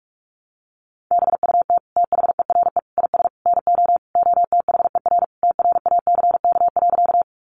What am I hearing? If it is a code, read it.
Morse code, heard as a keyed tone, about 44 wpm, 700 Hz. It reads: BUT THERE IS NO OTHER TRACK.
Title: Sentences from Top 1100 Words 45wpm